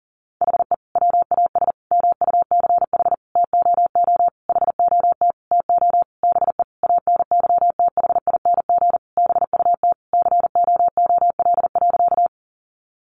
Morse code, heard as a keyed tone, about 40 wpm, 700 Hz. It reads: HE WAS MUCH TOO HOT TO BE ANYTHING BUT COOL.
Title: Sentences from Top 1200 Words 40wpm